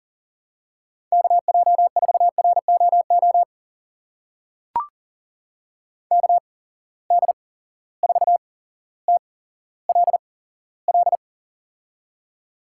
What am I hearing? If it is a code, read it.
Morse code, heard as a keyed tone, about 42 wpm, spaced as for 17 wpm, 700 Hz. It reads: KJ4ROO KD4TLL
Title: USA Call Signs - 6 Characters - 8x Character Spacing - 40wpm